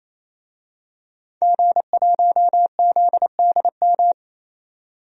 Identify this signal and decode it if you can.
Morse code, heard as a keyed tone, about 28 wpm, 700 Hz. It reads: G1ZDM
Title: DXCC-100 Call Signs - 5 Characters - 28wpm